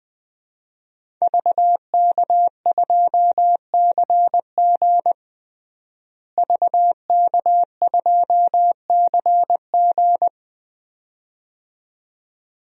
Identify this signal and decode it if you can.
Morse code, heard as a keyed tone, about 20 wpm, 700 Hz. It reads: VK2CG VK2CG
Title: DXCC-100 Call Signs - 5 Characters - Repeated 2x - 20wpm